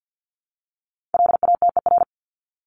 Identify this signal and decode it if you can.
Morse code, heard as a keyed tone, about 50 wpm, 700 Hz. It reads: LATER
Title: Top 501-750 Words 50wpm